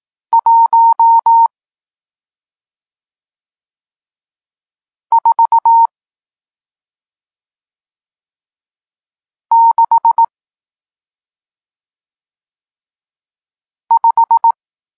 Morse code, heard as a keyed tone, about 18 wpm, 920 Hz. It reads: 1 4 6 5
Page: Tekst 40 – Nauka znaków przestankowych